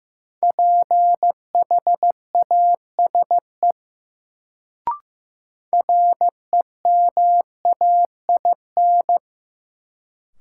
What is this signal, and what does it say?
Morse code, heard as a keyed tone, about 15 wpm, 700 Hz. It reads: PHASE REMAIN
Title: Sentences from Top 1200 Words - New Word Drill Alt - 15wpm